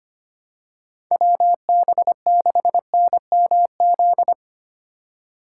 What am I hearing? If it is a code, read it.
Morse code, heard as a keyed tone, about 25 wpm, 700 Hz. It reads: WB6NMZ
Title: USA Call Signs - 6 Characters - 3x Character Spacing - 25wpm